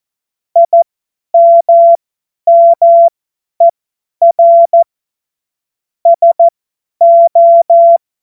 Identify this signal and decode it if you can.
Morse code, heard as a keyed tone, about 14 wpm, spaced as for 10 wpm, 680 Hz. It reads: IMMER SO
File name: Morsekurs / Witze / Witz 18. April 2023 50 F70